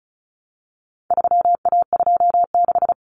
Morse code, heard as a keyed tone, about 34 wpm, 700 Hz. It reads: 3A26
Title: Common Cut Numbers - 4-Digit 35wpm